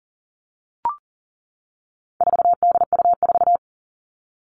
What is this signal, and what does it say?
Morse code, heard as a keyed tone, about 40 wpm, 700 Hz. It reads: 4DU4
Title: All Cut Numbers - 4-Digit 40wpm